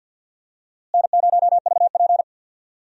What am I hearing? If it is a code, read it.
Morse code, heard as a keyed tone, about 46 wpm, 700 Hz. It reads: N0VP
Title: USA Call Signs - 4 Characters - Repeated 2x - 50wpm